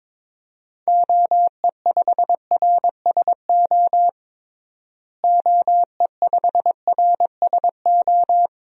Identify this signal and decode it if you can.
Morse code, heard as a keyed tone, about 21 wpm, 700 Hz. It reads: OE5RSO OE5RSO
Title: DXCC-100 Call Signs - 6 Characters - Repeated 2x - 22wpm